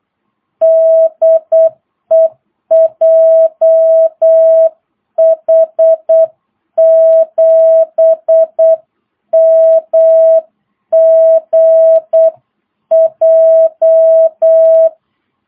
Morse code, heard as a keyed tone, about 8 wpm, 660 Hz. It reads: DE JH7MGJ
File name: jh8cbh / 2022cw